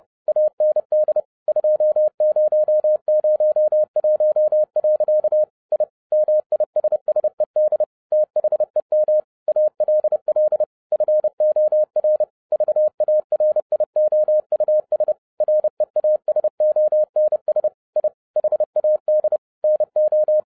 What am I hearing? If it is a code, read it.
Morse code, heard as a keyed tone, about 30 wpm, 610 Hz. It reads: AND 2001. I MISSED THEM ALL FOR VARIOUS REASONS I HAD NO
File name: 2012-01-mp3 / n11-I DIDNT MISS THE SOLAR CYCLE 24-30wpm